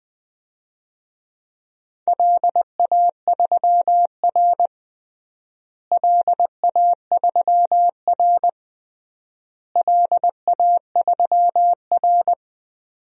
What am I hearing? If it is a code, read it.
Morse code, heard as a keyed tone, about 21 wpm, 700 Hz. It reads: LA3R LA3R LA3R
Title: DXCC-100 Call Signs - 4 Characters - Repeated 3x - 20wpm